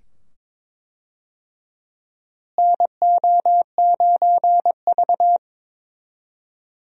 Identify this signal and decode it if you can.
Morse code, heard as a keyed tone, about 22 wpm, 700 Hz. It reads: NO9V
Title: USA Call Signs - 4 Characters - Repeated 2x - 22wpm